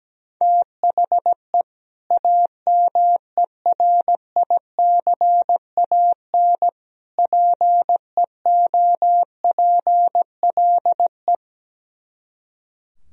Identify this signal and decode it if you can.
Morse code, heard as a keyed tone, about 17 wpm, 700 Hz. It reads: THE AMERICAN PEOPLE
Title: Speed-Racing - Sets of 3 Words 17wpm